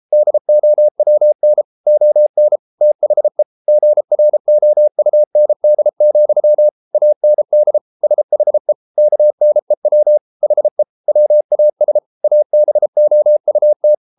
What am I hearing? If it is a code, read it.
Morse code, heard as a keyed tone, about 33 wpm, 600 Hz. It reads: DOWN ON THE GROUND, AND SHE KNEW HE WAS ABOUT